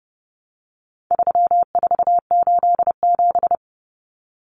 Morse code, heard as a keyed tone, about 30 wpm, 700 Hz. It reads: 3487
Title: Numbers - 4-Digit 30wpm